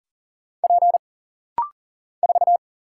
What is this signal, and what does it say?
Morse code, heard as a keyed tone, about 40 wpm, 700 Hz. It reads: P 4
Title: Single Letter-Number Alt - Rapid-Fire 40wpm